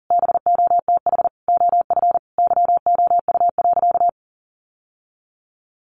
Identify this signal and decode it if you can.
Morse code, heard as a keyed tone, about 40 wpm, 700 Hz. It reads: BOTH OF YOU.
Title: Sentences from Top 1100 Words 40wpm